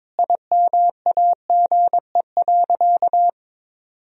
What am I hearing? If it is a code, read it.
Morse code, heard as a keyed tone, about 22 wpm, 700 Hz. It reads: IMAGE.
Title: Sentences from Top 1000 Words - Review Part 4 - 22wpm